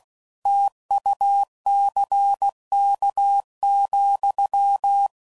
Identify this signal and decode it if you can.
Morse code, heard as a keyed tone, about 16 wpm, 800 Hz. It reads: TUCK,